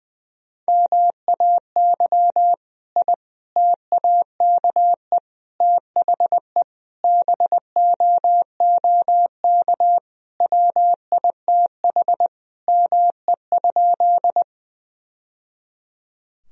Morse code, heard as a keyed tone, about 20 wpm, 700 Hz. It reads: MAY I TAKE THE BOOK WITH ME?
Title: Sentences from Top 1000 Words - Review Part 1 - 20wpm